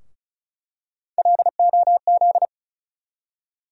Morse code, heard as a keyed tone, about 36 wpm, 700 Hz. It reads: LOZ
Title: USA Call Sign Postfixes - 3 Letters 35wpm